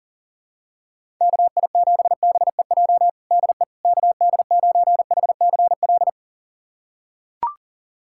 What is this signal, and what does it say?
Morse code, heard as a keyed tone, about 40 wpm, 700 Hz. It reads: KI7BEJ DE KD9HCL
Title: USA Callsign DE Callsign - 6 Characters - 40wpm